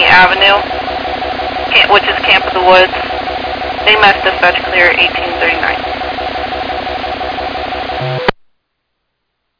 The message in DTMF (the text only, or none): none